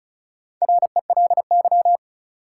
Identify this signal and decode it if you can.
Morse code, heard as a keyed tone, about 36 wpm, 700 Hz. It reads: RELY
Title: Sentences from Top 1200 Words - New Word Drill Alt - 35wpm